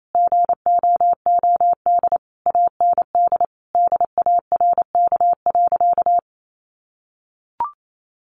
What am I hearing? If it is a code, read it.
Morse code, heard as a keyed tone, about 28 wpm, 700 Hz. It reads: GOOD AND DARK.
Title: Sentences from Top 600 Words 28wpm